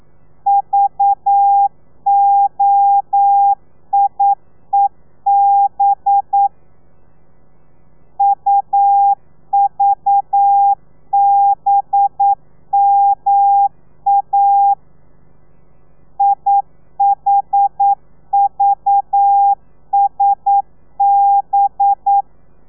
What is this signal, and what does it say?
Morse code, heard as a keyed tone, about 9 wpm, 800 Hz. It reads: VOIEB UVBMA IHVSB
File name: telegrafia / i2tfj / Lez4 40